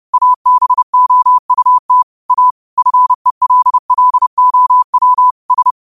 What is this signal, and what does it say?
Morse code, heard as a keyed tone, about 30 wpm, 1000 Hz. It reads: ABOUT A FELLOW S